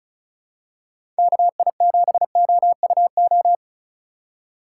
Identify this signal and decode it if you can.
Morse code, heard as a keyed tone, about 36 wpm, 700 Hz. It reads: KI7OUO
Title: USA Call Signs - 4x Character Spacing - 35wpm